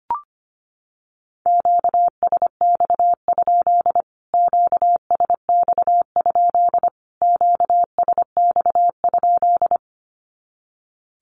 Morse code, heard as a keyed tone, about 25 wpm, 700 Hz. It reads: QSX? QSX? QSX?
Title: Q-Codes Repeated 3x 25wpm